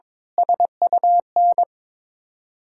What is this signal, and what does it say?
Morse code, heard as a keyed tone, about 22 wpm, 700 Hz. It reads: SUN
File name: Sentences with 7-Word Phrases - 22wpm